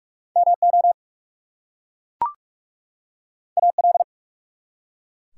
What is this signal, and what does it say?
Morse code, heard as a keyed tone, about 44 wpm, 700 Hz. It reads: MO AL
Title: US States and Canadian Provinces 45wpm